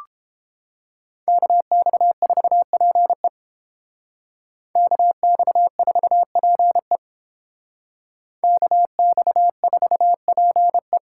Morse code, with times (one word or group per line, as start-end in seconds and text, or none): 1.3-3.3 KX4PE
4.8-7.0 KX4PE
8.4-11.0 KX4PE